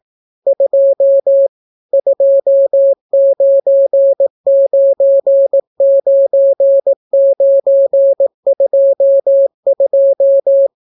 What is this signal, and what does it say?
Morse code, heard as a keyed tone, about 18 wpm, 550 Hz. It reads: 2 2999922